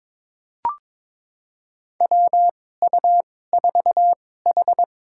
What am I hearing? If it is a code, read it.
Morse code, heard as a keyed tone, about 22 wpm, spaced as for 16 wpm, 700 Hz. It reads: WU4H
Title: USA Call Signs - 2x Character Spacing - 22wpm